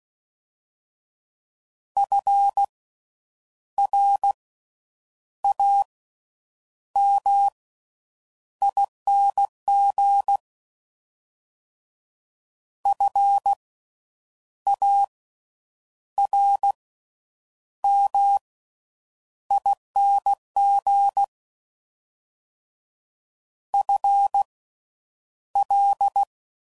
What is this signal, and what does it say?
Morse code, heard as a keyed tone, about 16 wpm, spaced as for 8 wpm, 800 Hz. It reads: FRAMING FARMING FL
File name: Slow With Faster Ings